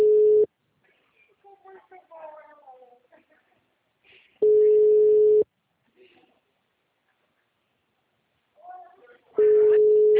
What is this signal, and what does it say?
Ringback tone (Europe, 425 Hz), with speech.
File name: out- -2009- - - .783